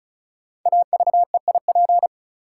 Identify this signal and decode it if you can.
Morse code, heard as a keyed tone, about 36 wpm, 700 Hz. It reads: AVEIP
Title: Code Groups - 5 Letters - 35wpm